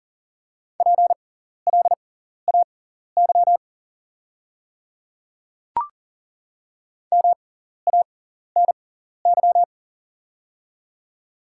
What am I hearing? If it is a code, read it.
Morse code, heard as a keyed tone, about 40 wpm, spaced as for 14 wpm, 690 Hz. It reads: PLAY MANY
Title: 6x Character Spacing - Top 500 Words - 40wpm